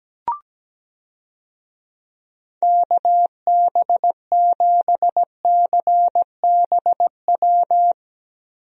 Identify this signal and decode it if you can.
Morse code, heard as a keyed tone, about 17 wpm, 700 Hz. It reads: KB7CBW
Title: USA Call Signs - 6 Characters - Repeated 3x - 17wpm